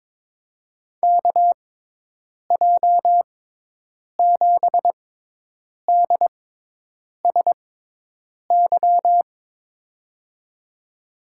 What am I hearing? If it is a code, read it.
Morse code, heard as a keyed tone, about 22 wpm, spaced as for 8 wpm, 700 Hz. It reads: KJ7DSY
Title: USA Call Signs - 6 Characters - 6x Character Spacing - 22wpm